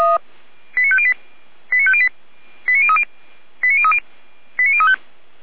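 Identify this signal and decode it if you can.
Five-tone selective call (ZVEI), heard as quick runs of five tones, tones of about 70 ms. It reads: 89508 89508 89030 89030 89035
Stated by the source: ZVEI five-tone selective call received by SDR.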